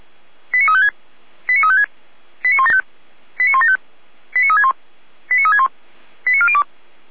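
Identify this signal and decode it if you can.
Five-tone selective call (ZVEI), heard as quick runs of five tones, tones of about 70 ms. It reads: 89356 89356 89175 89175 89361 89361 89402
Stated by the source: ZVEI five-tone selective call received by SDR.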